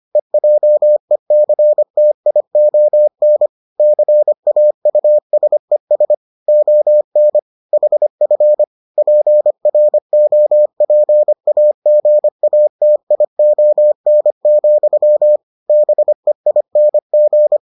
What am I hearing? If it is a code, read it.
Morse code, heard as a keyed tone, about 25 wpm, 600 Hz. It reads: EJECTION CAUSES ON HF PROPAGATION, BEING